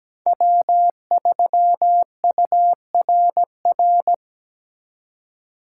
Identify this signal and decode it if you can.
Morse code, heard as a keyed tone, about 20 wpm, 700 Hz. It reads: W3URR
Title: Speed Racing - USA Call Signs 17wpm